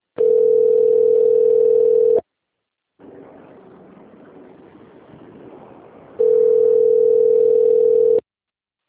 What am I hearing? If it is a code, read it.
Ringback tone (North America), 440 + 480 Hz.
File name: rg-1600-unknown- - - .8958